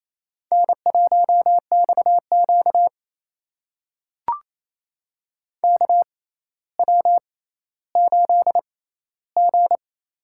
Morse code, heard as a keyed tone, about 28 wpm, spaced as for 14 wpm, 700 Hz. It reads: N1XQ KW8G
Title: USA Call Signs - 4 Characters - 6x Character Spacing - 28wpm